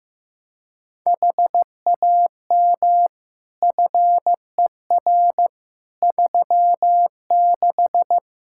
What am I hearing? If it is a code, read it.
Morse code, heard as a keyed tone, about 15 wpm, 700 Hz. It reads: HAM FER 36
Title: Practice QSO - Exchange 3 - More Info - 15wpm